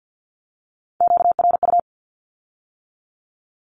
Morse code, heard as a keyed tone, about 50 wpm, 700 Hz. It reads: QRU
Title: Q-Codes with Spoken Meaning - 50wpm